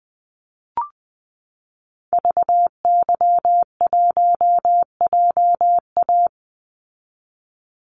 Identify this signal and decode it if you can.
Morse code, heard as a keyed tone, about 20 wpm, 700 Hz. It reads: VY1JA
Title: Call Signs Repeated 3x - 20wpm